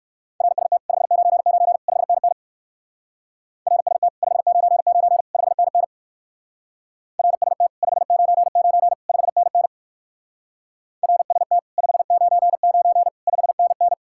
Morse code, heard as a keyed tone, about 57 wpm, 700 Hz. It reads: RST 599 5NN RST 599 5NN RST 599 5NN RST 599 5NN